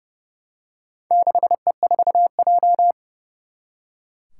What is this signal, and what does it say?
Morse code, heard as a keyed tone, about 30 wpm, 700 Hz. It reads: 6E4J
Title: Four Challenging Letter-Number Combinations 30wpm